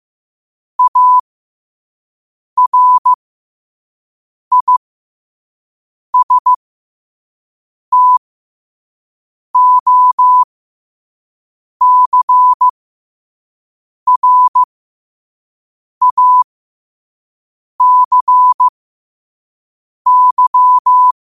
Morse code, heard as a keyed tone, about 15 wpm, spaced as for 5 wpm, 1000 Hz. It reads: ARISTOCRACY
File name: qotdfiles / 5wpm